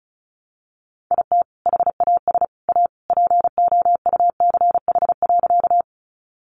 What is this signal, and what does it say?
Morse code, heard as a keyed tone, about 34 wpm, 700 Hz. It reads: IT HAS A POUCH.
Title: Easy Phrases 35wpm